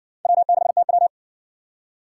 Morse code, heard as a keyed tone, about 60 wpm, 700 Hz. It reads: W6TW